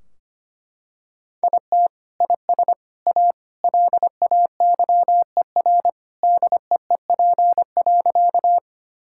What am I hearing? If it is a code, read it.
Morse code, heard as a keyed tone, about 25 wpm, 700 Hz. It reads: IT IS A LAYER DEEP.
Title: Sentences from Top 1600 Words 25wpm